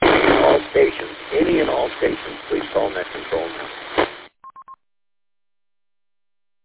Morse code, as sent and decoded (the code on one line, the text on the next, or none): none